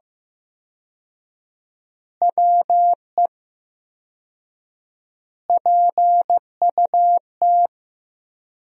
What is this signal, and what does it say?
Morse code, heard as a keyed tone, about 15 wpm, 700 Hz. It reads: WE PUT
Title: 4x Word-Spacing - Sentences from Top 200 Words - 15wpm